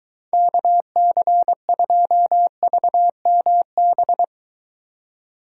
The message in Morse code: KC2VMB

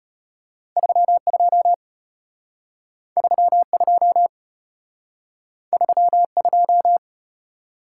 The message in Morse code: 32 32 32

